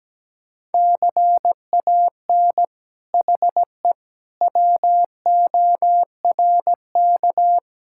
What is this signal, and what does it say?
Morse code, heard as a keyed tone, about 17 wpm, 700 Hz. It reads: CAN HE WORK
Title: Sentences from Top 1000 Words - Review Part 6 - 17wpm